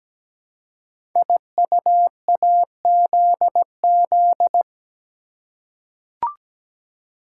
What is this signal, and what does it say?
Morse code, heard as a keyed tone, about 17 wpm, 700 Hz. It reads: IUAZZ
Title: Code Groups - 5 Letters - 17wpm